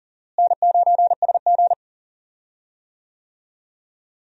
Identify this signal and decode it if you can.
Morse code, heard as a keyed tone, about 40 wpm, 700 Hz. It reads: N9SG
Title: USA Call Signs - 4 Characters - Repeated 2x - 40wpm